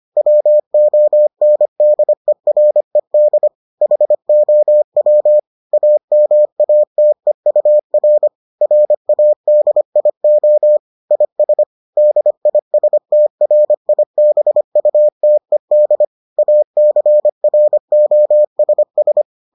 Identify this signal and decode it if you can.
Morse code, heard as a keyed tone, about 25 wpm, 600 Hz. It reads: WONDERED HOW AMATEUR RADIO IS DISTRIBUTED ACROSS